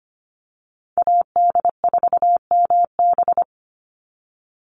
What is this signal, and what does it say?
Morse code, heard as a keyed tone, about 25 wpm, 700 Hz. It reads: AD4MB
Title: USA Call Signs 25wpm